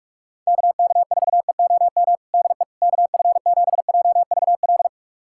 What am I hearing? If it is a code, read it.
Morse code, heard as a keyed tone, about 46 wpm, 690 Hz. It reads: KK4EOM DE KF7JVL
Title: USA Callsign DE Callsign - 6 Characters - 45wpm